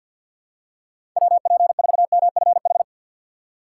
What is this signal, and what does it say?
Morse code, heard as a keyed tone, about 52 wpm, 700 Hz. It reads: WP4MFH
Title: USA Call Signs - 3x Character Spacing - 50wpm